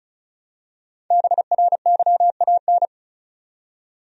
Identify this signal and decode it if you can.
Morse code, heard as a keyed tone, about 36 wpm, 700 Hz. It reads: BRYAN
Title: USA Common Names - 35wpm